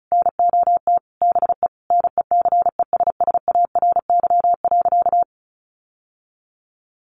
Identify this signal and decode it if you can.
Morse code, heard as a keyed tone, about 34 wpm, 700 Hz. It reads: NOT BE NECESSARY.
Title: Sentences from Top 800 Words 35wpm